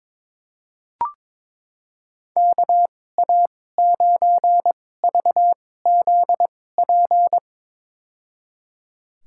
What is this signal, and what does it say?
Morse code, heard as a keyed tone, about 21 wpm, spaced as for 16 wpm, 700 Hz. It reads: KA9VZP
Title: USA Call Signs - 6 Characters - 2x Character Spacing - 22wpm